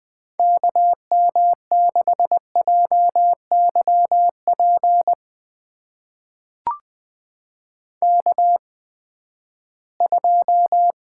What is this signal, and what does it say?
Morse code, heard as a keyed tone, about 21 wpm, spaced as for 14 wpm, 700 Hz. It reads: KM6JYP K2
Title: USA Call Signs - 8x Character Spacing - 20wpm